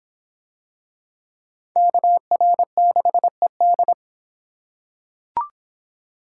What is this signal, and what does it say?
Morse code, heard as a keyed tone, about 24 wpm, 700 Hz. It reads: KR6ED